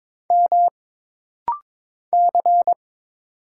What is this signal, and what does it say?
Morse code, heard as a keyed tone, about 22 wpm, 700 Hz. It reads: M C
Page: This practice set is composed of Letters, Numbers, common punctuations, and common prosigns.